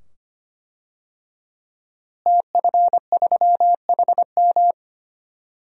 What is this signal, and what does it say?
Morse code, heard as a keyed tone, about 25 wpm, 700 Hz. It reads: TF3HM
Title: DXCC-100 Call Signs - 5 Characters - Repeated 2x - 25wpm